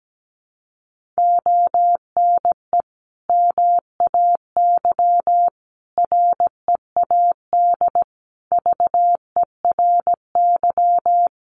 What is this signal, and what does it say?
Morse code, heard as a keyed tone, about 17 wpm, 700 Hz. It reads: ONE MAY READ VERY
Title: Sentences from Top 1100 Words 17wpm